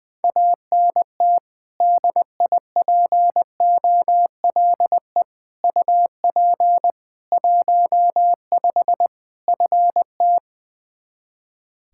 Morse code, heard as a keyed tone, about 19 wpm, 700 Hz. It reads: ANT DIPOLE UP 15 FT
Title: Practice QSO - Exchange 2 - Info - 20wpm